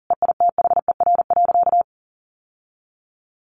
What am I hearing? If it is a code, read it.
Morse code, heard as a keyed tone, about 40 wpm, 700 Hz. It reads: EITHER.
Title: Sentences from Top 1200 Words 40wpm